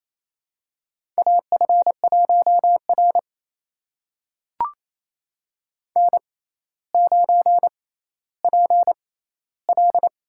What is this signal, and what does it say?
Morse code, heard as a keyed tone, about 28 wpm, spaced as for 14 wpm, 700 Hz. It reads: AF1R N9PL
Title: USA Call Signs - 4 Characters - 6x Character Spacing - 28wpm